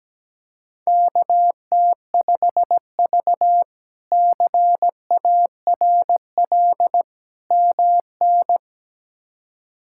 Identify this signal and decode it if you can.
Morse code, heard as a keyed tone, about 17 wpm, 700 Hz. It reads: KT5V CARL MN
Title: K1USN SST QSO Practice - 17wpm